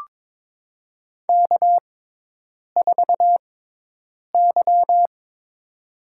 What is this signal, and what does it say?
Morse code, heard as a keyed tone, about 23 wpm, spaced as for 8 wpm, 700 Hz. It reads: K4Y
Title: USA Call Signs - 4 Characters - 6x Character Spacing - 22wpm